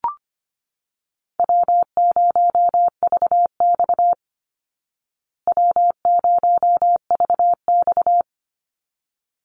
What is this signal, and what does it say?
Morse code, heard as a keyed tone, about 25 wpm, 700 Hz. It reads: W0VX W0VX
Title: Easy Call Signs Repeated 2x - 25wpm